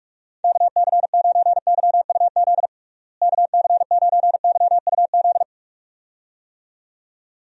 Morse code, heard as a keyed tone, about 46 wpm, 700 Hz. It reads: KC9YUZ KC9YUZ
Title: USA Call Signs - 6 Characters - Repeated 2x - 45wpm